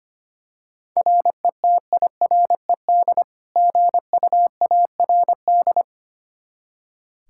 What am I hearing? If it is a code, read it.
Morse code, heard as a keyed tone, about 25 wpm, 700 Hz. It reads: RETIRED GUARD
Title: K1USN SST QSO Practice 25wpm